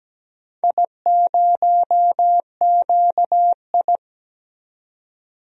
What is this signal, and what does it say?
Morse code, heard as a keyed tone, about 17 wpm, 700 Hz. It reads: I0QI
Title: DXCC-100 Call Signs - 4 Characters - 17wpm